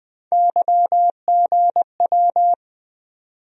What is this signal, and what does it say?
Morse code, heard as a keyed tone, about 21 wpm, 700 Hz. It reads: YGW
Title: USA Call Sign Postfixes - 3 Letters 20wpm